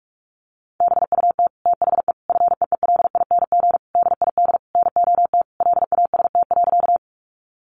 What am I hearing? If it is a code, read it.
Morse code, heard as a keyed tone, about 44 wpm, 700 Hz. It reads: BUT THE FEELING DID NOT LAST.
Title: Sentences from Top 1200 Words 45wpm